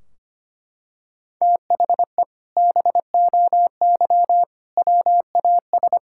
Morse code, heard as a keyed tone, about 25 wpm, 700 Hz. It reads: THE BOY WAS
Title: Sentences from Top 1600 Words 25wpm